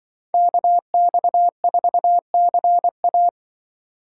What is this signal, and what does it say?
Morse code, heard as a keyed tone, about 25 wpm, 700 Hz. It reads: KX4CA
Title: Speed Racing - USA 2x2 Call Signs 20wpm